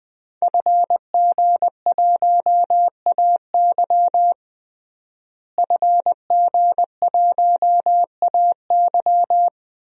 Morse code, heard as a keyed tone, about 21 wpm, 700 Hz. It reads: FG1AY FG1AY
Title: DXCC-100 Call Signs - 5 Characters - Repeated 3x - 20wpm